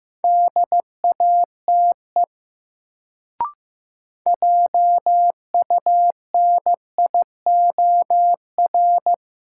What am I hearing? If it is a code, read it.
Morse code, heard as a keyed tone, about 15 wpm, 700 Hz. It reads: DATE JUNIOR
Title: Sentences from Top 1000 Words - Review Part 7 - New Word Drill - 15wpm